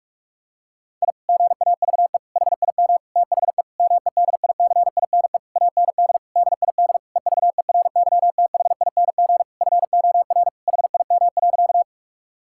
Morse code, heard as a keyed tone, about 48 wpm, 700 Hz. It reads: I GAVE HIM THE MEDICINE AND DID EVERYTHING FOR HIM.
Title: Sentences from Top 1000 Words - Review Part 6 - 45wpm